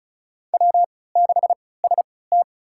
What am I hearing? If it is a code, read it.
Morse code, heard as a keyed tone, about 36 wpm, spaced as for 20 wpm, 700 Hz. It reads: W6ST
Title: USA Call Signs - 4 Characters - 3x Character Spacing - 35wpm